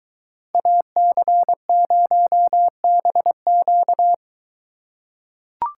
Morse code, heard as a keyed tone, about 22 wpm, 700 Hz. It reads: AC0BQ